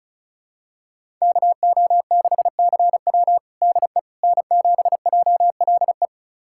Morse code, heard as a keyed tone, about 36 wpm, 700 Hz. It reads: KO6CW DE N7JLE
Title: USA Callsign DE Callsign - 5 Characters - 35wpm